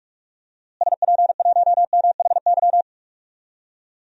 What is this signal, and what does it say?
Morse code, heard as a keyed tone, about 46 wpm, 700 Hz. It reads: SP1MHY
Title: DXCC-100 Call Signs - 6 Characters - Repeated 3x - 45wpm